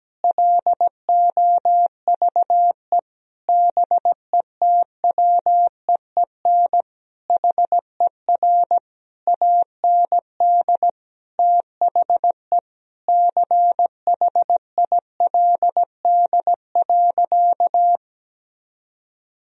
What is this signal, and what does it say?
Morse code, heard as a keyed tone, about 17 wpm, 700 Hz. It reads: LOVE BETWEEN HER AND THE CHILD.
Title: Sentences from Top 1000 Words - Review Part 1 - 17wpm